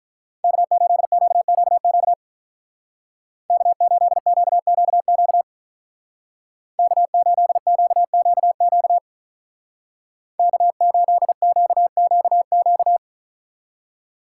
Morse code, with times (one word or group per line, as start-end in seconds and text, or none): 0.4-2.1 K8QQQ
3.5-5.4 K8QQQ
6.8-9.0 K8QQQ
10.4-13.0 K8QQQ